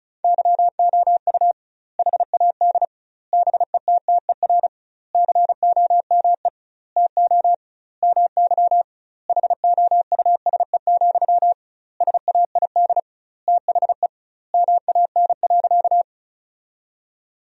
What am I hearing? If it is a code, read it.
Morse code, heard as a keyed tone, about 36 wpm, 700 Hz. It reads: YOU HAD BETTER COME TO MY HOUSE, SAID THE MAN.
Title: 2x Word-Spacing - Sentences from Top 400 Words - 35wpm